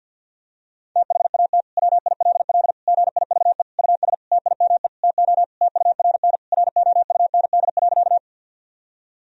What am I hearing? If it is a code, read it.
Morse code, heard as a keyed tone, about 52 wpm, 700 Hz. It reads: THAT WILL GIVE US TIME TO TURN ROUND.
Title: Sentences from Top 1000 Words - Review Part 1 - 50wpm